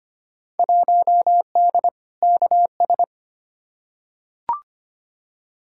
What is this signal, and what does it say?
Morse code, heard as a keyed tone, about 25 wpm, 700 Hz. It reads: 1D KS
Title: Field Day Exchanges - 25wpm